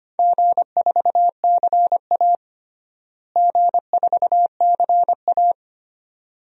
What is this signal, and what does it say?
Morse code, heard as a keyed tone, about 25 wpm, 700 Hz. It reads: G4CA G4CA
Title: DXCC-100 Call Signs - 4 Characters - Repeated 2x - 25wpm